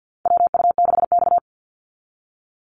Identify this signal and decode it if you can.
Morse code, heard as a keyed tone, about 50 wpm, 700 Hz. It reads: WU6X
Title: Call Signs - 50wpm